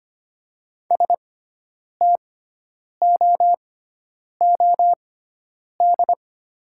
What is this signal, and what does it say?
Morse code, heard as a keyed tone, about 25 wpm, spaced as for 9 wpm, 700 Hz. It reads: STOOD
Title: 6x Character Spacing - Top 500 Words - 25wpm